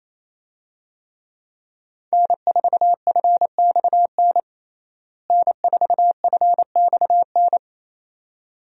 Morse code, heard as a keyed tone, about 28 wpm, 700 Hz. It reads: N4FXN N4FXN
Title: USA Call Signs - 5 Characters - Repeated 2x - 28wpm